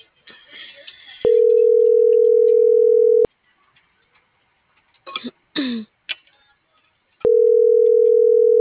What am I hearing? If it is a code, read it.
Ringback tone (North America), 440 + 480 Hz.